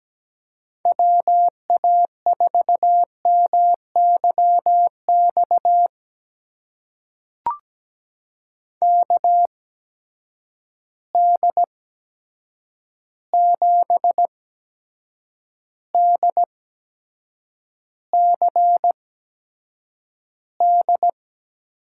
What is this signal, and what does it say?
Morse code, heard as a keyed tone, about 17 wpm, spaced as for 7 wpm, 700 Hz. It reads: WA4MYX KD7DCD
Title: USA Call Signs - 6 Characters - 8x Character Spacing - 17wpm